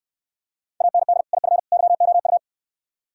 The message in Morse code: AND IF YOU

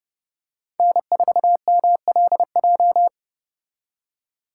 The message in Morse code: N4MLJ

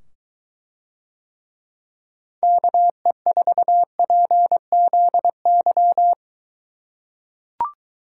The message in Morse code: KE4PZY

